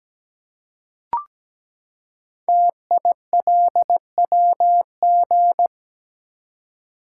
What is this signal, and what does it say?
Morse code, heard as a keyed tone, about 17 wpm, 700 Hz. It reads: TILWG
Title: Code Groups - 5 Letters - 17wpm